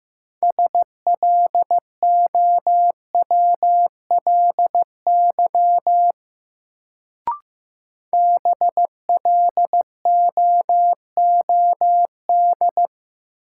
Morse code, heard as a keyed tone, about 15 wpm, 700 Hz. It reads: SLOWLY BLOOD
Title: Sentences from Top 1100 Words - New Word Drill Alt - 15wpm